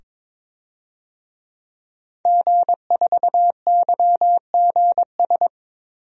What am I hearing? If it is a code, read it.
Morse code, heard as a keyed tone, about 22 wpm, 700 Hz. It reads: G4YGS
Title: DXCC-100 Call Signs - 5 Characters - 22wpm